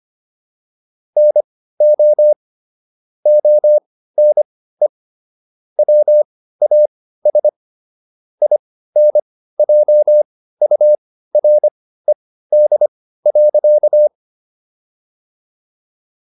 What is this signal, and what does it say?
Morse code, heard as a keyed tone, about 25 wpm, spaced as for 15 wpm, 600 Hz. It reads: NO ONE WAS INJURED.